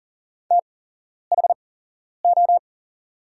Morse code, heard as a keyed tone, about 40 wpm, spaced as for 11 wpm, 690 Hz. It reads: THO
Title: 8x Character Spacing - Top 500 Words - 40wpm